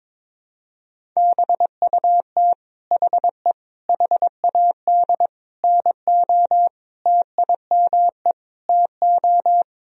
Morse code, heard as a keyed tone, about 22 wpm, 700 Hz. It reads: BUT HE HAD NO TIME TO
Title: Sentences from Top 1000 Words - Review Part 1 - 22wpm